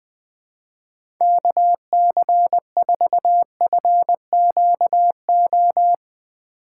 Morse code, heard as a keyed tone, about 20 wpm, 700 Hz. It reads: KC4FQO
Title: USA Call Signs - 6 Characters - 2x Character Spacing - 20wpm